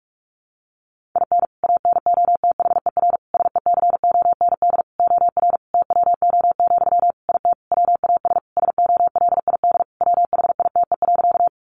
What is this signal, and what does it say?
Morse code, heard as a keyed tone, about 44 wpm, 700 Hz. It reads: IN ANOTHER SECOND OR TWO, IT WAS SOLID WHITE.
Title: Sentences from Top 1500 Words 45wpm